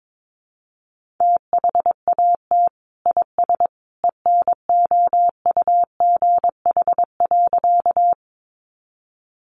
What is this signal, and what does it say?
Morse code, heard as a keyed tone, about 22 wpm, 700 Hz. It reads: THAT IS ENOUGH.
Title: Sentences from Top 400 Words 22wpm